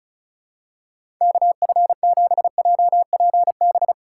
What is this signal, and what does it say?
Morse code, heard as a keyed tone, about 34 wpm, 700 Hz. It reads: KF7JPB
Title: USA Call Signs - 2x Character Spacing - 35wpm